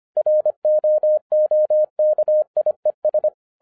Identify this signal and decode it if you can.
Morse code, heard as a keyed tone, about 25 wpm, 610 Hz. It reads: ROOKIES
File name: 2011-07-mp3 / n24 QRP OTAKU 25wpm